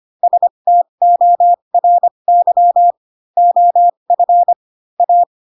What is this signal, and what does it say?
Morse code, heard as a keyed tone, about 25 wpm, spaced as for 22 wpm, 700 Hz. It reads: STORY OF A
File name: Pinocchio / Pinocchio ch7 18 25 ik1hge